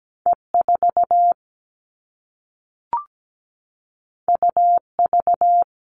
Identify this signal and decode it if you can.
Morse code, heard as a keyed tone, about 17 wpm, 700 Hz. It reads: E4 UV
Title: All Cut Numbers - 2-Digit 17wpm